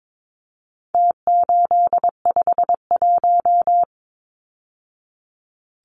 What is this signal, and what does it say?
Morse code, heard as a keyed tone, about 22 wpm, 700 Hz. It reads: T851
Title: Numbers - Contest Style - 4-Digit 22wpm